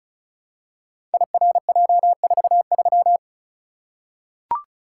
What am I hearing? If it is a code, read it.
Morse code, heard as a keyed tone, about 36 wpm, 700 Hz. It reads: IRJ43
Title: Code Groups - 5 Letter:Numbers - 35wpm